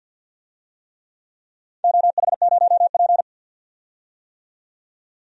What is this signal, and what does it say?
Morse code, heard as a keyed tone, about 48 wpm, 700 Hz. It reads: OH0P
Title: DXCC-100 Call Signs - 4 Characters - 50wpm